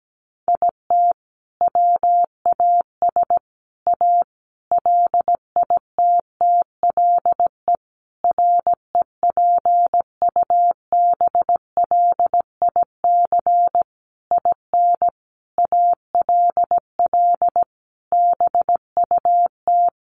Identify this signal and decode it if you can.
Morse code, heard as a keyed tone, about 17 wpm, 700 Hz. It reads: IT WAS A LITTLE REPUBLIC IN ALL BUT
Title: Sentences from Top 1500 Words 17wpm